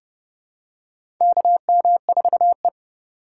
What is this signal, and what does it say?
Morse code, heard as a keyed tone, about 30 wpm, 700 Hz. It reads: KM4E
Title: USA Call Signs - 4 Characters - 4x Character Spacing - 30wpm